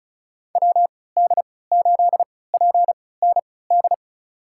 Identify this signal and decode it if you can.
Morse code, heard as a keyed tone, about 36 wpm, spaced as for 20 wpm, 700 Hz. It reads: WD8PND
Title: USA Call Signs - 6 Characters - 3x Character Spacing - 35wpm